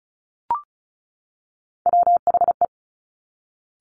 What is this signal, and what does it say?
Morse code, heard as a keyed tone, about 36 wpm, 700 Hz. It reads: WHE
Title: Top Three Letter Combinations in Words 35wpm